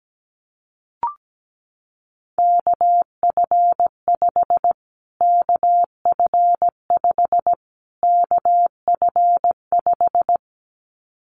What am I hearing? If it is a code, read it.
Morse code, heard as a keyed tone, about 17 wpm, 700 Hz. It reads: KF5 KF5 KF5
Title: Callsign Prefixes for Top 25 DXCC - Repeated 3x - 17wpm